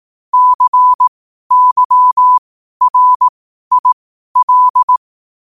Morse code, heard as a keyed tone, about 18 wpm, spaced as for 12 wpm, 1000 Hz. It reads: CYRIL
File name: qotdfiles / 12wpm